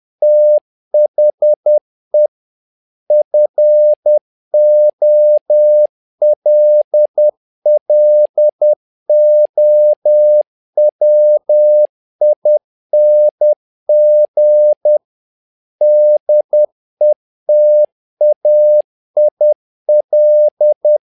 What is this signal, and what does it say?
Morse code, heard as a keyed tone, about 10 wpm, 600 Hz. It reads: THE FOLLOWING DETAIL
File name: learning center / tower trouble / tower trouble 10wpm